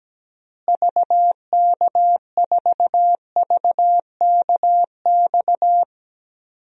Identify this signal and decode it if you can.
Morse code, heard as a keyed tone, about 17 wpm, 700 Hz. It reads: VK4VKX
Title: DXCC-100 Call Signs - 6 Characters - Repeated 2x - 17wpm